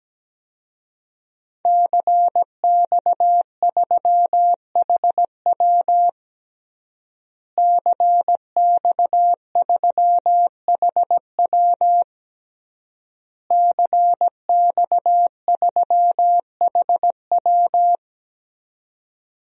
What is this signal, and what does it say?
Morse code, heard as a keyed tone, about 17 wpm, 700 Hz. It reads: CX3HW CX3HW CX3HW
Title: DXCC-100 Call Signs - 5 Characters - Repeated 3x - 17wpm